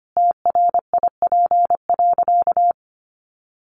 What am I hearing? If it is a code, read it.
Morse code, heard as a keyed tone, about 25 wpm, 700 Hz. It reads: TRIP.
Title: Sentences from Top 1300 Words 25wpm